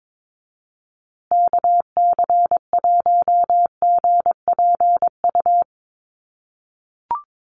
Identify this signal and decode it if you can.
Morse code, heard as a keyed tone, about 22 wpm, 700 Hz. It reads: KC1GPU
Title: USA Call Signs Repeated 2x 22wpm